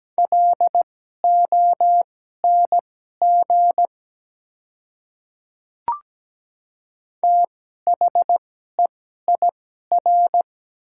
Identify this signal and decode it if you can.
Morse code, heard as a keyed tone, about 17 wpm, spaced as for 12 wpm, 690 Hz. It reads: LONG THEIR
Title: 2x Character Spacing - Top 100 Words - 17wpm